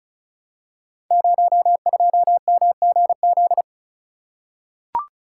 Morse code, heard as a keyed tone, about 36 wpm, 700 Hz. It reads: 02MGZ
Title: Code Groups - 5 Letter:Numbers - 35wpm